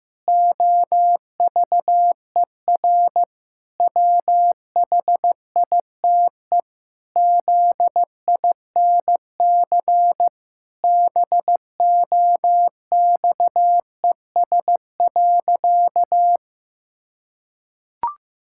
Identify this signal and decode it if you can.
Morse code, heard as a keyed tone, about 14 wpm, 700 Hz. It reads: OVER WHITE ZINC BOXES.
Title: Pangram Challenge v2 - 15wpm